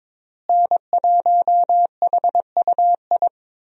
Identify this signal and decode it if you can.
Morse code, heard as a keyed tone, about 22 wpm, 700 Hz. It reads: N1HUI
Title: USA Call Signs - 5 Characters - Repeated 3x - 22wpm